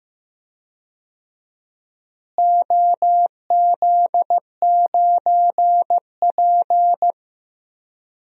Morse code, heard as a keyed tone, about 15 wpm, 700 Hz. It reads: OZ9P
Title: DXCC-100 Call Signs - 4 Characters - Repeated 3x - 15wpm